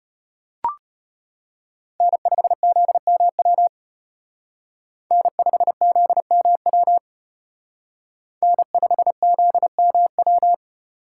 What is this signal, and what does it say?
Morse code, heard as a keyed tone, about 29 wpm, 700 Hz. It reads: N5ZMW N5ZMW N5ZMW
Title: Speed Racing - USA 1x3 Call Signs 25wpm